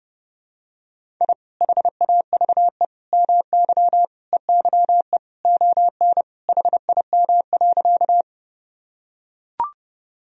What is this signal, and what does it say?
Morse code, heard as a keyed tone, about 30 wpm, 700 Hz. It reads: I HAVE MY EYE ON HIM.
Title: Sentences from Top 1000 Words - Review Part 1 - 30wpm